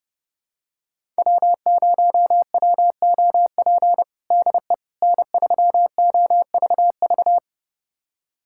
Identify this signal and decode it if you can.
Morse code, heard as a keyed tone, about 30 wpm, 700 Hz. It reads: W0WOP DE N3OVV
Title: USA Callsign DE Callsign - 5 Characters - 30wpm